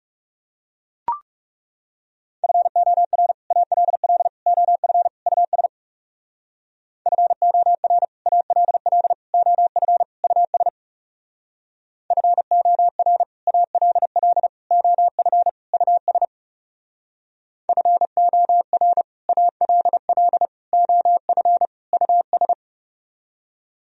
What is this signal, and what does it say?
Morse code, heard as a keyed tone, about 38 wpm, 700 Hz. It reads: FOR ALL OF US FOR ALL OF US FOR ALL OF US FOR ALL OF US